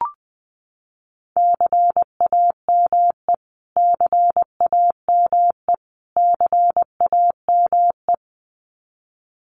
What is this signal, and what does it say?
Morse code, heard as a keyed tone, about 20 wpm, 700 Hz. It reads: CAME CAME CAME
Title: Top 500 Words Repeated 20wpm